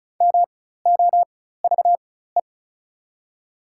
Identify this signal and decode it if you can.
Morse code, heard as a keyed tone, about 34 wpm, spaced as for 16 wpm, 700 Hz. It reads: MOVE
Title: 4x Character Spacing - Top 500 Words - 35wpm